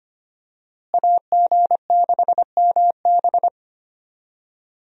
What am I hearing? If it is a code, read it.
Morse code, heard as a keyed tone, about 25 wpm, 700 Hz. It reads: AG6MB
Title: USA Call Signs - 5 Characters - Repeated 2x - 25wpm